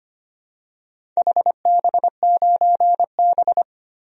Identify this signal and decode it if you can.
Morse code, heard as a keyed tone, about 25 wpm, 700 Hz. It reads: HB9B
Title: DXCC-100 Call Signs - 4 Characters - 25wpm